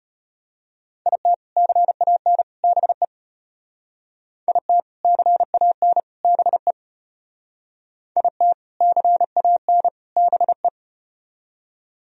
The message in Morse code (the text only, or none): IT CAN BE IT CAN BE IT CAN BE